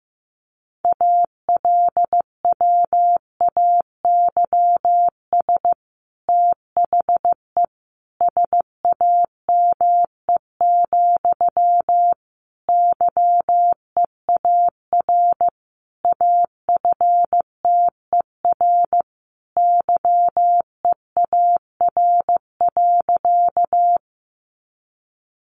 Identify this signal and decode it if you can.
Morse code, heard as a keyed tone, about 15 wpm, 700 Hz. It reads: ALWAYS THE SAME, YEAR AFTER YEAR.
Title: Sentences from Top 300 Words 15wpm